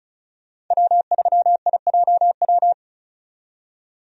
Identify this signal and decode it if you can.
Morse code, heard as a keyed tone, about 36 wpm, 700 Hz. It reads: W3IJW
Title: USA Call Signs - 5 Characters - 35wpm